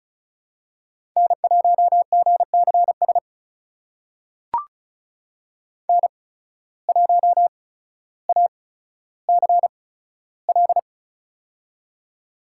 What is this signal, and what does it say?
Morse code, heard as a keyed tone, about 34 wpm, spaced as for 15 wpm, 700 Hz. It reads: N1GCS N1ACL
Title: USA Call Signs - 5 Characters - 8x Character Spacing - 35wpm